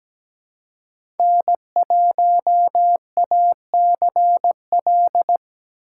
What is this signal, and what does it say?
Morse code, heard as a keyed tone, about 17 wpm, 700 Hz. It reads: N1ACL
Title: USA Call Signs - 5 Characters - 6x Character Spacing - 17wpm